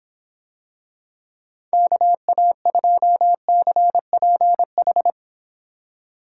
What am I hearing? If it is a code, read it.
Morse code, heard as a keyed tone, about 26 wpm, 700 Hz. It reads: KA2CPH